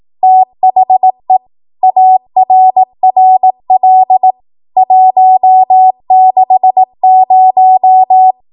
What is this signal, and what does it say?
Morse code, heard as a keyed tone, about 18 wpm, 750 Hz. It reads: THE ARRL 160
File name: Archive / 18 WPM / 18WPM